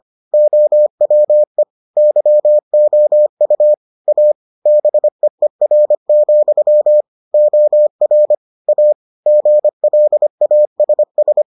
Morse code, heard as a keyed tone, about 25 wpm, 600 Hz. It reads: OWE YOU A BEER, OR A GLASS